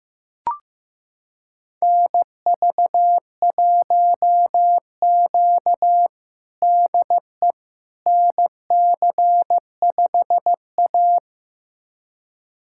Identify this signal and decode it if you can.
Morse code, heard as a keyed tone, about 15 wpm, 700 Hz. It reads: NV1Q DE NC5A
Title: USA Callsign DE Callsign - 4 Characters - 15wpm